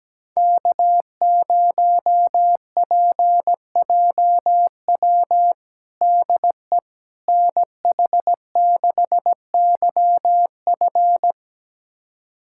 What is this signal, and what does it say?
Morse code, heard as a keyed tone, about 17 wpm, 700 Hz. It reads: K0PJW DE NH6YF
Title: USA Callsign DE Callsign - 5 Characters - 17wpm